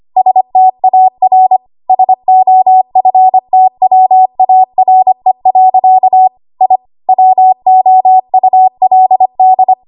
Morse code, heard as a keyed tone, about 25 wpm, 750 Hz. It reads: STAR SOFTWARE. I WOULD
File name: Archive / 25 WPM / 25WPM